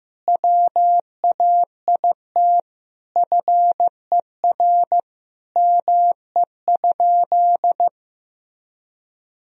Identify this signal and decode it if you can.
Morse code, heard as a keyed tone, about 15 wpm, 700 Hz. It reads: WAIT FER ME?
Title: Sentences from Top 500 Words and Abbreviations 15wpm